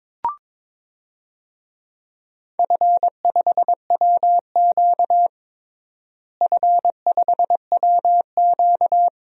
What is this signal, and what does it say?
Morse code, heard as a keyed tone, about 22 wpm, 700 Hz. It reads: F5WQ F5WQ
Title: DXCC-100 Call Signs - 4 Characters - Repeated 3x - 22wpm